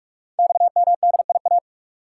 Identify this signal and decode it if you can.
Morse code, heard as a keyed tone, about 44 wpm, 690 Hz. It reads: XMDIA
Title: Code Groups - 5 Letters - 45wpm